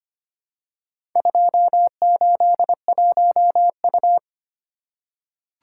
Morse code, heard as a keyed tone, about 25 wpm, 700 Hz. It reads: 281U
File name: practice / 25wpm / Four Letter-Number Combinations - 25wpm